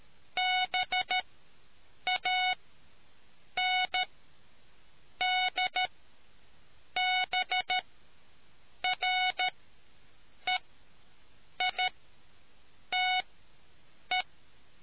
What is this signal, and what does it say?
Morse code, heard as a keyed tone, about 13 wpm, spaced as for 6 wpm, 740 Hz. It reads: BANDBREITE